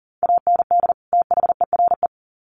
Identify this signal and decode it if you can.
Morse code, heard as a keyed tone, about 40 wpm, 700 Hz. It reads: AND THERE
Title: Sets of 2 Words 40wpm